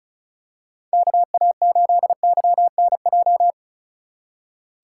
Morse code, heard as a keyed tone, about 34 wpm, 690 Hz. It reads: KA8YNJ
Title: USA Call Signs - 6 Characters - 2x Character Spacing - 35wpm